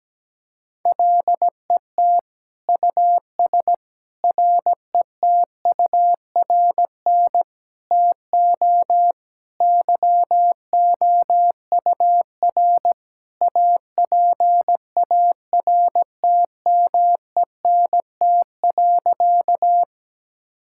Morse code, heard as a keyed tone, about 17 wpm, 700 Hz. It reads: LET US RETURN TO YOUR APARTMENT.
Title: Sentences from Top 1000 Words - Review Part 6 - 17wpm